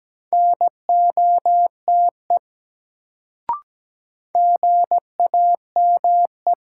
Morse code, heard as a keyed tone, about 17 wpm, 700 Hz. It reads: NOTE GAME
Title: Sentences from Top 500 Words - New Word Drill - 17wpm